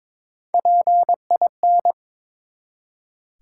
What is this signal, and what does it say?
Morse code, heard as a keyed tone, about 23 wpm, 700 Hz. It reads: PIN
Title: Three Letter Words - 22wpm